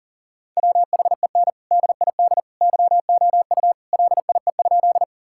Morse code, heard as a keyed tone, about 40 wpm, 700 Hz. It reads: WHEN DID YOU LIE?
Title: Sentences from Top 1600 Words 40wpm